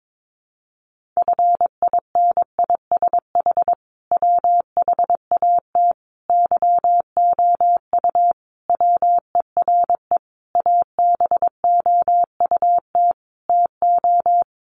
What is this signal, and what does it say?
Morse code, heard as a keyed tone, about 22 wpm, 700 Hz. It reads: FINISH WHAT YOU WERE ABOUT TO
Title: Sentences from Top 1200 Words 22wpm